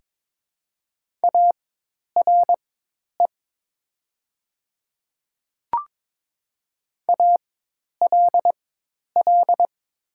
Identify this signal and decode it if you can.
Morse code, heard as a keyed tone, about 22 wpm, spaced as for 10 wpm, 700 Hz. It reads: ARE ALL
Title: 4x Character Spacing - Top 100 Words - 22wpm